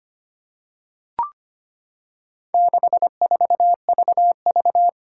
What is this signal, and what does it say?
Morse code, heard as a keyed tone, about 25 wpm, 700 Hz. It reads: 64VV
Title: Four Challenging Letter-Number Combinations 25wpm